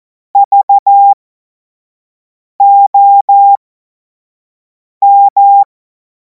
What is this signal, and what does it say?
Morse code, heard as a keyed tone, about 14 wpm, spaced as for 5 wpm, 800 Hz. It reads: VOM
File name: Morsekurs / Witze / Witz 03. Februar 2026 25 F70